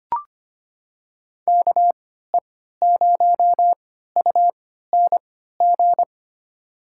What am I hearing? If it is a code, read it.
Morse code, heard as a keyed tone, about 25 wpm, spaced as for 14 wpm, 700 Hz. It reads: KE0UNG
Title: USA Call Signs - 6 Characters - 3x Character Spacing - 25wpm